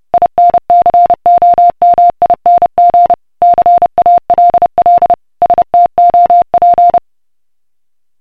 Morse code, heard as a keyed tone, about 30 wpm, 700 Hz. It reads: INCOMING CALL STOP